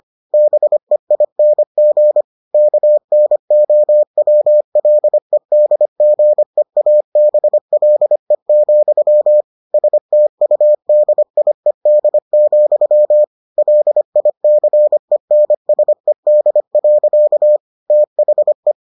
Morse code, heard as a keyed tone, about 25 wpm, 600 Hz. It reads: BEING KNOWLEDGEABLE, STUDIED, LICENSED. THE